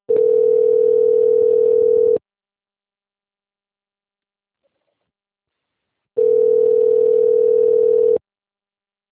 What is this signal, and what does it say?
Ringback tone (North America), 440 + 480 Hz.